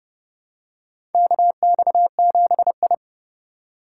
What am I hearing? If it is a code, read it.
Morse code, heard as a keyed tone, about 30 wpm, 700 Hz. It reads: KX7I
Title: USA Call Signs - 4 Characters - 2x Character Spacing - 30wpm